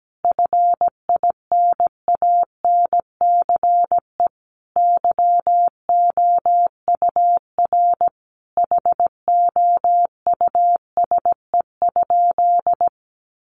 Morse code, heard as a keyed tone, about 17 wpm, 700 Hz. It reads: FINANCE YOUR HOUSE?
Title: Sentences from Top 1500 Words 17wpm